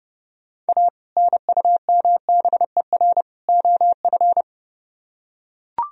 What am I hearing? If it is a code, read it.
Morse code, heard as a keyed tone, about 30 wpm, 700 Hz. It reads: A NUMBER OF